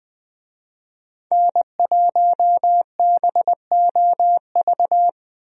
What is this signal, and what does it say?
Morse code, heard as a keyed tone, about 20 wpm, 700 Hz. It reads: N1BOV
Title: USA Call Signs - 5 Characters - Repeated 2x - 20wpm